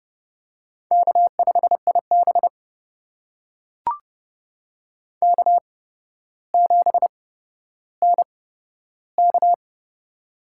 Morse code, heard as a keyed tone, about 30 wpm, spaced as for 13 wpm, 700 Hz. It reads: K5IB K7NK
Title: USA Call Signs - 4 Characters - 8x Character Spacing - 30wpm